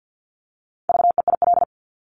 Morse code, heard as a keyed tone, about 50 wpm, 700 Hz. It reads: VEIL
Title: Multi-letter Phonemes Repeated 3x 50wpm